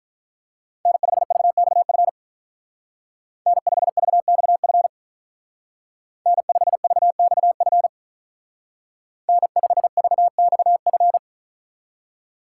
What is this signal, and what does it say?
Morse code, heard as a keyed tone, about 43 wpm, 700 Hz. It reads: N5VXF N5VXF N5VXF N5VXF
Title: Speed Racing - USA Call Signs 35wpm